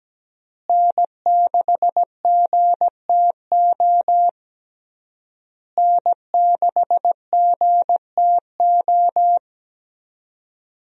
Morse code, heard as a keyed tone, about 17 wpm, 700 Hz. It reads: N6GTO N6GTO
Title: USA Call Signs - 5 Characters - Repeated 3x - 17wpm